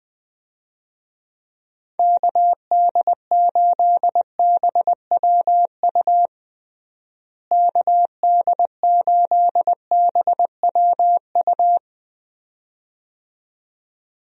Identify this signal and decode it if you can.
Morse code, heard as a keyed tone, about 20 wpm, 700 Hz. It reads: KD8BWU KD8BWU
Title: USA Call Signs - 6 Characters - Repeated 2x - 20wpm